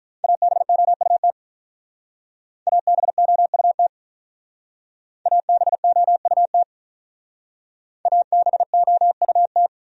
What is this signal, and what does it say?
Morse code, heard as a keyed tone, about 46 wpm, 700 Hz. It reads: ABOUT ABOUT ABOUT ABOUT